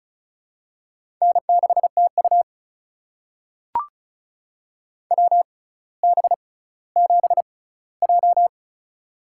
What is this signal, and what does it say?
Morse code, heard as a keyed tone, about 36 wpm, spaced as for 18 wpm, 700 Hz. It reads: N6TU WB7J
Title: USA Call Signs - 4 Characters - 6x Character Spacing - 35wpm